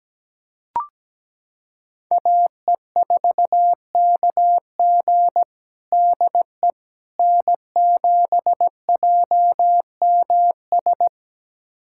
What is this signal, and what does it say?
Morse code, heard as a keyed tone, about 17 wpm, 700 Hz. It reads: AE4KG DE N7JMS
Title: USA Callsign DE Callsign - 5 Characters - 17wpm